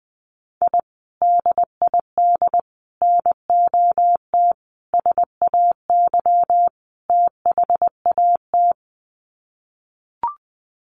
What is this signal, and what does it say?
Morse code, heard as a keyed tone, about 20 wpm, 700 Hz. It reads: I DID NOT SAY THAT
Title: Sets of 5 Words Encore2 20wpm